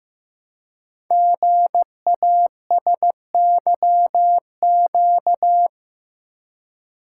Morse code, heard as a keyed tone, about 15 wpm, 700 Hz. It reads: GASYQ
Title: Code Groups - 5 Letter:Numbers - 15wpm